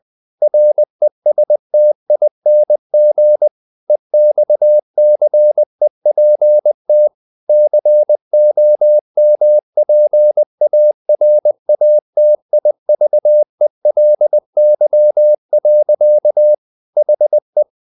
Morse code, heard as a keyed tone, about 20 wpm, 600 Hz. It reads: RESTING EXCEPT COMPARATIVELY. HE
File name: the old man and the sea part 2 20wpm